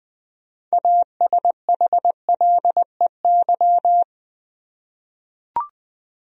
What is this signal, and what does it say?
Morse code, heard as a keyed tone, about 20 wpm, 700 Hz. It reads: ASHLEY
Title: USA Common Names - 20wpm